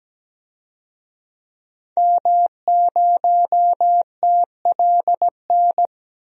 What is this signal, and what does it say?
Morse code, heard as a keyed tone, about 17 wpm, 700 Hz. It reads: M0TLN
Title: DXCC-100 Call Signs - 5 Characters - 17wpm